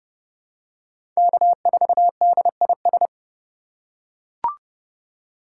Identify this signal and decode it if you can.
Morse code, heard as a keyed tone, about 32 wpm, 700 Hz. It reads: K4DIS
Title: USA Call Signs - 5 Characters - 3x Character Spacing - 30wpm